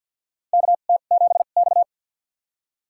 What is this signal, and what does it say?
Morse code, heard as a keyed tone, about 49 wpm, spaced as for 35 wpm, 700 Hz. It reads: KT7X
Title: USA Call Signs - 4 Characters - 2x Character Spacing - 50wpm